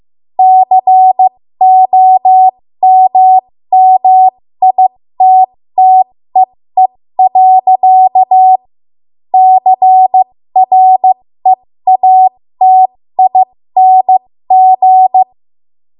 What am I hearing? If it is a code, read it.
Morse code, heard as a keyed tone, about 15 wpm, spaced as for 13 wpm, 750 Hz. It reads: COMMITTEE. CREATING
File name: Morse / 13WPM